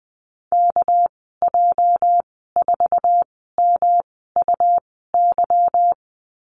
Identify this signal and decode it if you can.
Morse code, heard as a keyed tone, about 20 wpm, spaced as for 15 wpm, 700 Hz. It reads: KJ4MUY
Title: 213 Lesson 34g - Number 8 - Call Signs 20wpm